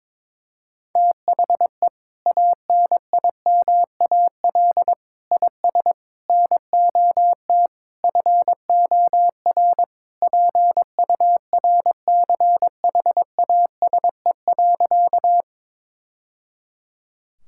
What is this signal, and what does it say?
Morse code, heard as a keyed tone, about 22 wpm, 700 Hz. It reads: THE ANIMAL IS NOT FOR PURCHASE.
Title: Sentences from Top 1000 Words - Review Part 5 - 22wpm